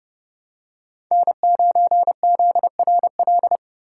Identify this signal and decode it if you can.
Morse code, heard as a keyed tone, about 29 wpm, 700 Hz. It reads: N9ZRL